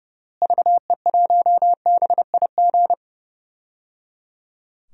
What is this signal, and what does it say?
Morse code, heard as a keyed tone, about 30 wpm, 700 Hz. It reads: VE1BIG
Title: DXCC-100 Call Signs - 6 Characters - Repeated 3x - 30wpm